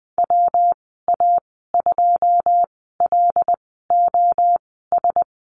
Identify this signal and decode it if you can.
Morse code, heard as a keyed tone, about 21 wpm, spaced as for 15 wpm, 700 Hz. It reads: WA2LOS
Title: 122 Lesson 20g - Number 3 - Call Signs 20wpm